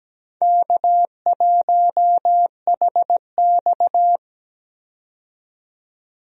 Morse code, heard as a keyed tone, about 17 wpm, 700 Hz. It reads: K1HX
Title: USA Call Signs - 4 Characters - 17wpm